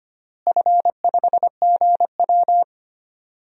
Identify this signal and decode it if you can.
Morse code, heard as a keyed tone, about 25 wpm, 700 Hz. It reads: F5GW
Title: DXCC-100 Call Signs - 4 Characters - Repeated 2x - 25wpm